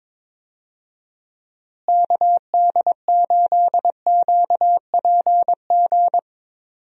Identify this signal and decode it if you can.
Morse code, heard as a keyed tone, about 22 wpm, 700 Hz. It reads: KD8QPG
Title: USA Call Signs - 6 Characters - Repeated 3x - 22wpm